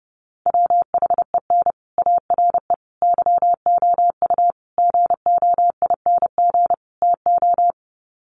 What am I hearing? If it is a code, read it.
Morse code, heard as a keyed tone, about 30 wpm, 700 Hz. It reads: WHEN ARE YOU GOING TO
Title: Sets of 5 Words Encore2 30wpm